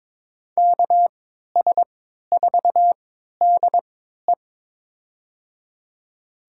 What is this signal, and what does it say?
Morse code, heard as a keyed tone, about 22 wpm, spaced as for 12 wpm, 700 Hz. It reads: KS4DE
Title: USA Call Signs - 5 Characters - 3x Character Spacing - 22wpm